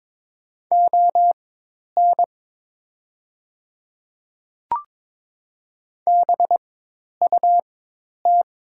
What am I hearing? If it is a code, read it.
Morse code, heard as a keyed tone, about 22 wpm, spaced as for 10 wpm, 700 Hz. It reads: ON BUT
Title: 4x Character Spacing - Top 100 Words - 22wpm